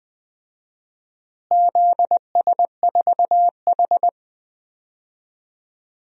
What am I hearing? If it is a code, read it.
Morse code, heard as a keyed tone, about 20 wpm, 700 Hz. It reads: ZS4H
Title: DXCC-100 Call Signs - 4 Characters - Repeated 3x - 20wpm